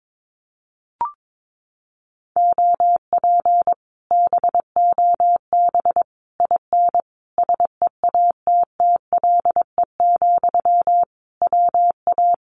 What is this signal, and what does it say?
Morse code, heard as a keyed tone, about 22 wpm, 700 Hz. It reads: OP BOB IN SEATTLE, WA
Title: Common QSO Phrases Encore 22wpm